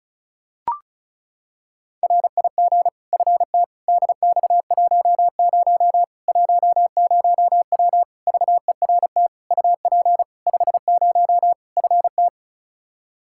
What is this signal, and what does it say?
Morse code, heard as a keyed tone, about 36 wpm, 700 Hz. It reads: RIG FT DX10 10W VERT UP 50 FT
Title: Practice QSO - Exchange 2 - Info - 35wpm